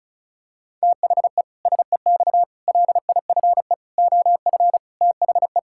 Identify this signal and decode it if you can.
Morse code, heard as a keyed tone, about 36 wpm, 700 Hz. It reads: THE SEX LIFE OF THE
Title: Pangram Challenge v2 - 35wpm